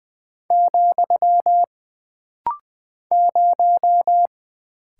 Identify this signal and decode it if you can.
Morse code, heard as a keyed tone, about 20 wpm, 700 Hz. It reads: , 0
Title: Single Character (Letter:Number:Punctuation:Prosign) - Rapid-Fire - 20wpm